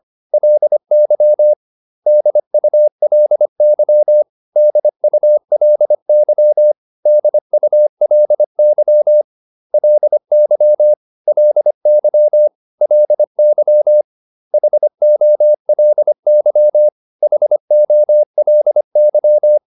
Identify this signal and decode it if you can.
Morse code, heard as a keyed tone, about 25 wpm, 600 Hz. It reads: LY DULY DULY DULY LY LY LY HOLY HOLY